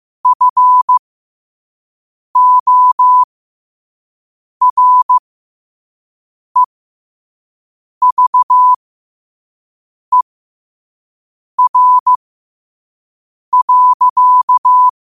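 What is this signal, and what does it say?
Morse code, heard as a keyed tone, about 15 wpm, spaced as for 5 wpm, 1000 Hz. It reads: FOREVER.
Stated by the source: Morse code quote of the day at 5 wpm.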